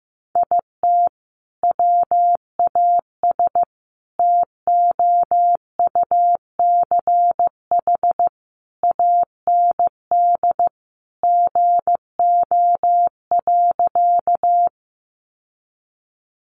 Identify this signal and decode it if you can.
Morse code, heard as a keyed tone, about 15 wpm, 700 Hz. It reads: IT WAS TOUCH AND GO.
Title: Sentences from Top 1200 Words 15wpm